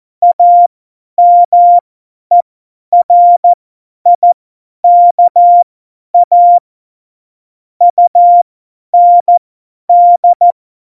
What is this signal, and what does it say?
Morse code, heard as a keyed tone, about 14 wpm, spaced as for 10 wpm, 700 Hz. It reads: AMERIKA UND
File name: Morsekurs / Witze / Witz 04. Juni 2019, 50 F70